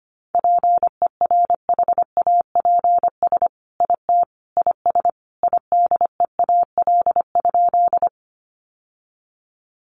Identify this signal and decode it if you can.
Morse code, heard as a keyed tone, about 25 wpm, 700 Hz. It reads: PERHAPS IT IS IDEAL?
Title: Sentences from Top 1400 Words 25wpm